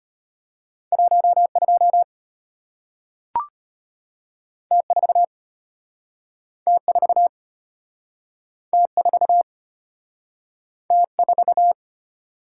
Morse code, heard as a keyed tone, about 34 wpm, 700 Hz. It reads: 12 T4 T4 T4 T4